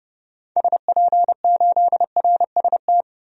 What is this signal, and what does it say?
Morse code, heard as a keyed tone, about 30 wpm, 700 Hz. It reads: SP8RST
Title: DXCC-100 Call Signs - 6 Characters - 30wpm